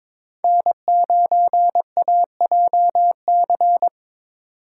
Morse code, heard as a keyed tone, about 22 wpm, 700 Hz. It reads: N9AJC
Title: USA Call Signs - 3x Character Spacing - 22wpm